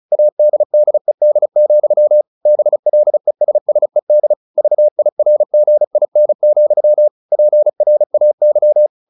Morse code, heard as a keyed tone, about 34 wpm, 600 Hz. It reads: ADDED, BLESSED VIRGIN, PRAY
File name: the old man and the sea / the old man and the sea 1 35wpm